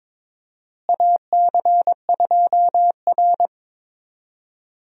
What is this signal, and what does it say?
Morse code, heard as a keyed tone, about 21 wpm, 700 Hz. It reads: AC2R
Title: USA Call Signs - 4 Characters - Repeated 2x - 22wpm